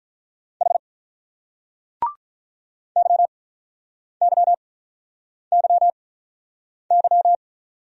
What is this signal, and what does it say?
Morse code, heard as a keyed tone, about 43 wpm, spaced as for 8 wpm, 700 Hz. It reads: H YYYY
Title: Speed-Racing Single Letters 35wpm